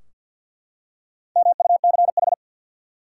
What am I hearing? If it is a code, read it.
Morse code, heard as a keyed tone, about 53 wpm, 700 Hz. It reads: MUCH
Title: Four Letter Words - 50wpm